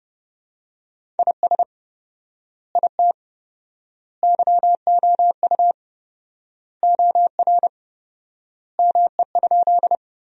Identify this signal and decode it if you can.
Morse code, heard as a keyed tone, about 30 wpm, 700 Hz. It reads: IS IT YOU OR ME?
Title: 4x Word-Spacing - Sentences from Top 200 Words - 30wpm